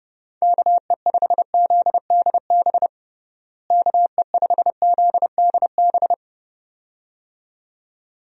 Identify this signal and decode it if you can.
Morse code, heard as a keyed tone, about 30 wpm, 700 Hz. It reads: KE5ZDB KE5ZDB
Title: USA Call Signs - 6 Characters - Repeated 2x - 30wpm